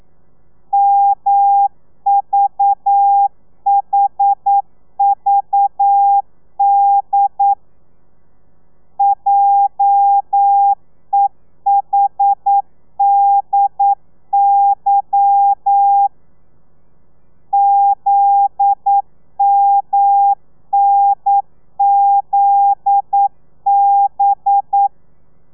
Morse code, heard as a keyed tone, about 9 wpm, 800 Hz. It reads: MVHVD JEHDY ZMNZB
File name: telegrafia / i2tfj / Lez11 40